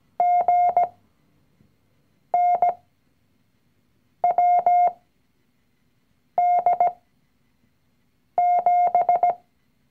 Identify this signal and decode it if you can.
Morse code, heard as a keyed tone, about 17 wpm, spaced as for 5 wpm, 710 Hz. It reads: GNWD7
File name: CW-Online-Kurs / Anfänger / Lektion 019